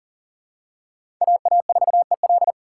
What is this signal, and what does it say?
Morse code, heard as a keyed tone, about 40 wpm, 700 Hz. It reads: AA4EL
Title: USA Call Signs - 3x Character Spacing - 40wpm